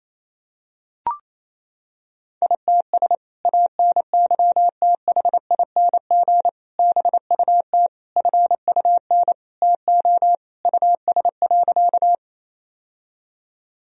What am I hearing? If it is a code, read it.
Morse code, heard as a keyed tone, about 28 wpm, 700 Hz. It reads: ITS ANYTHING BUT FUN TO US.
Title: Sentences from Top 1000 Words - Review Part 4 - 28wpm